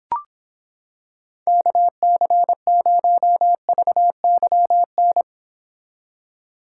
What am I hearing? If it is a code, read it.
Morse code, heard as a keyed tone, about 26 wpm, 700 Hz. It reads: KC0VYN